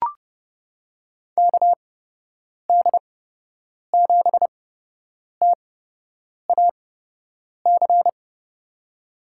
Morse code, heard as a keyed tone, about 30 wpm, spaced as for 8 wpm, 700 Hz. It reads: KD7TAC
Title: USA Call Signs - 6 Characters - 8x Character Spacing - 30wpm